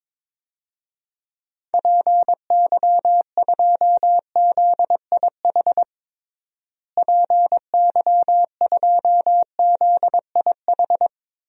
Morse code, heard as a keyed tone, about 21 wpm, 700 Hz. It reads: PY2ZIH PY2ZIH
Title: DXCC-100 Call Signs - 6 Characters - Repeated 3x - 22wpm